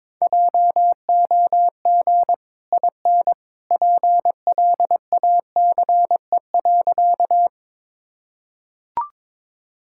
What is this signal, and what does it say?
Morse code, heard as a keyed tone, about 22 wpm, 700 Hz. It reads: JOG IN PLACE.
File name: practice / 22wpm / Easy Sentences with Hints - 22wpm